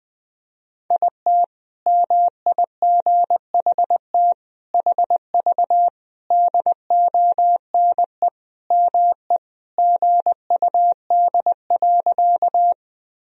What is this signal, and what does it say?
Morse code, heard as a keyed tone, about 20 wpm, 700 Hz. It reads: IT MIGHT HV DONE ME GUD.
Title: Sentences from Top 500 Words and Abbreviations 20wpm